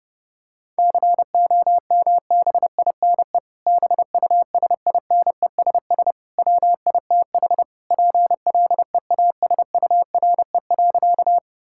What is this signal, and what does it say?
Morse code, heard as a keyed tone, about 32 wpm, 700 Hz. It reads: COMBINE BUSINESS WITH PLEASURE.
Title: Sentences from Top 1600 Words 30wpm